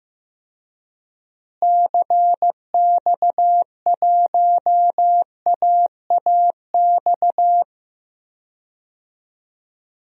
Morse code, heard as a keyed tone, about 16 wpm, 700 Hz. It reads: CX1AAX
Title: DXCC-100 Call Signs - 6 Characters - Repeated 3x - 15wpm